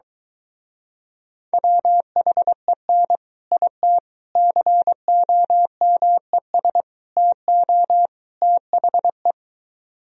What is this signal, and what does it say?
Morse code, heard as a keyed tone, about 23 wpm, 700 Hz. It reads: WHEN IT COMES TO THE